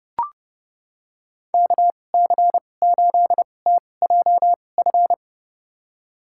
Morse code, heard as a keyed tone, about 30 wpm, spaced as for 22 wpm, 700 Hz. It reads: KC8TJF
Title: USA Call Signs - 6 Characters - 2x Character Spacing - 30wpm